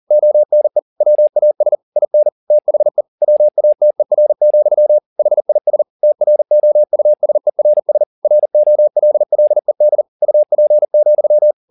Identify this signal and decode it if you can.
Morse code, heard as a keyed tone, about 40 wpm, 600 Hz. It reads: ONE WAS IN THE WATER, HIS TROUSERS ROLLED UP,
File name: the old man and the sea 2 40wpm